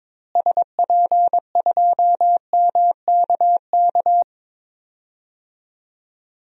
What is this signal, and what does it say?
Morse code, heard as a keyed tone, about 22 wpm, 700 Hz. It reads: SP2MKK
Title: DXCC-100 Call Signs - 6 Characters - Repeated 3x - 22wpm